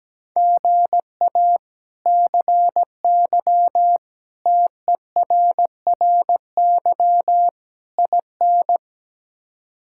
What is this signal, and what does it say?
Morse code, heard as a keyed tone, about 17 wpm, 700 Hz. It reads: GA CY TERRY IN
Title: K1USN SST QSO Practice - 17wpm